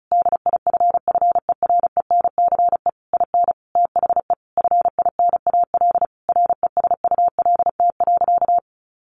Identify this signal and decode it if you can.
Morse code, heard as a keyed tone, about 34 wpm, 700 Hz. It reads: DIFFERENCE IN THE FINAL RESULT.
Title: Sentences from Top 800 Words 35wpm